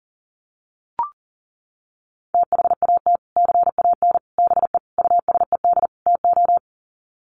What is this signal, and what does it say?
Morse code, heard as a keyed tone, about 40 wpm, 700 Hz. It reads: THAT CAN BE USED TO
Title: Sets of 5 Words Encore 40wpm